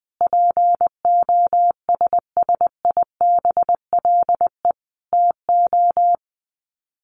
Morse code, heard as a keyed tone, about 20 wpm, 700 Hz. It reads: POSSIBLE TO
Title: Sets of 5 Words Encore2 20wpm